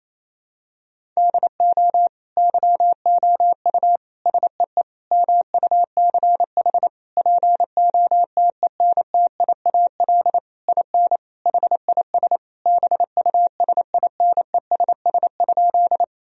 Morse code, heard as a keyed tone, about 28 wpm, 700 Hz. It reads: DO YOU SEE MUCH POTENTIAL IN HIS BUSINESS?
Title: Sentences from Top 1000 Words - Review Part 4 - 28wpm